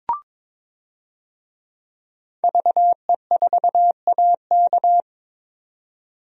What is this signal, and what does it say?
Morse code, heard as a keyed tone, about 22 wpm, 700 Hz. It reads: VE4AK
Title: DXCC-100 Call Signs - 5 Characters - 22wpm